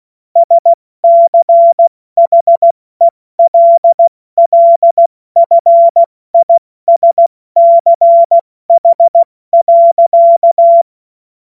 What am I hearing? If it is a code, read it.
Morse code, heard as a keyed tone, about 16 wpm, 680 Hz. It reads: SCHELLFISCH.
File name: Morsekurs / Witze / Witz 08. August 2023 70 F80